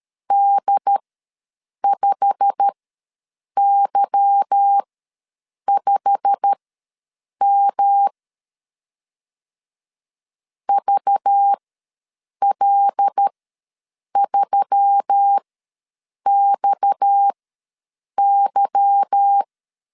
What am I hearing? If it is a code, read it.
Morse code, heard as a keyed tone, about 13 wpm, spaced as for 7 wpm, 800 Hz. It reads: D5Y5M VL3XY
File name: LEZIONI / 8 YJ45 / 8 YJ45 ws 12wpm 3146